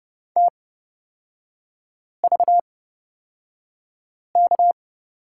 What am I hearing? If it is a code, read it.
Morse code, heard as a keyed tone, about 30 wpm, spaced as for 5 wpm, 700 Hz. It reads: TVK
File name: Lesson 38e - Letter q - Three Characters 30wpm